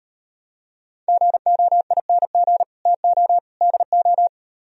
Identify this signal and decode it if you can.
Morse code, heard as a keyed tone, about 38 wpm, 700 Hz. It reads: GOING TO DO